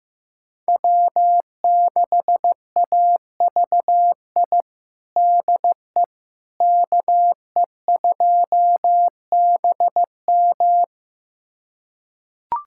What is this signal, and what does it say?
Morse code, heard as a keyed tone, about 15 wpm, 700 Hz. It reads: W6AVI DE KE2BM
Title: USA Callsign DE Callsign - 5 Characters - 15wpm